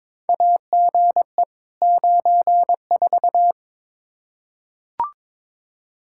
Morse code, heard as a keyed tone, about 22 wpm, 700 Hz. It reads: AGE 94
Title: K1USN SST QSO Practice 22wpm